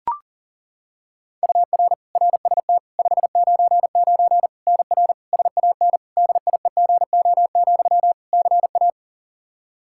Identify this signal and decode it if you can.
Morse code, heard as a keyed tone, about 37 wpm, 700 Hz. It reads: UR RST 599 NR SAN DIEGO, CA
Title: Practice QSO - Exchange 1 - Beginning - 40wpm